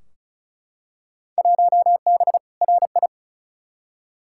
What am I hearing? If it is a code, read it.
Morse code, heard as a keyed tone, about 36 wpm, 700 Hz. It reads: 1B RI
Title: Field Day Exchanges Encore - 35wpm